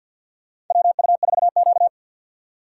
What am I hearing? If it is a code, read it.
Morse code, heard as a keyed tone, about 47 wpm, 700 Hz. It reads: WU4Q
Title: USA Call Signs - 4 Characters - 6x Character Spacing - 50wpm